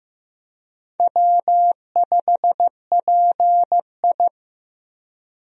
Morse code, heard as a keyed tone, about 15 wpm, 700 Hz. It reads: W5PI
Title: USA Call Signs - 4 Characters - 15wpm